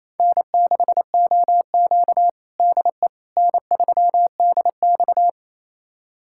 Morse code, heard as a keyed tone, about 28 wpm, 700 Hz. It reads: N6OQ DE N3DX
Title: USA Callsign DE Callsign - 4 Characters - 28wpm